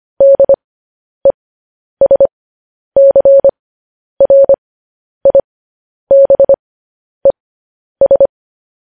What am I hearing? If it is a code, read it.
Morse code, heard as a keyed tone, about 25 wpm, spaced as for 10 wpm, 550 Hz. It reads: DESCRIBES
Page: news headlines in morse code at 10 wpm